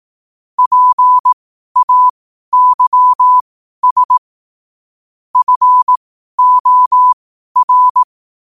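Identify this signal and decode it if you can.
Morse code, heard as a keyed tone, about 18 wpm, spaced as for 12 wpm, 1000 Hz. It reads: PAYS FOR
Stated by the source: Morse code quote practice at 12 words per minute.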